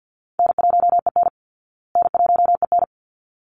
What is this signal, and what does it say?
Morse code, heard as a keyed tone, about 50 wpm, 700 Hz. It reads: N1EN N1EN
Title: Easy Call Signs Repeated 2x - 50wpm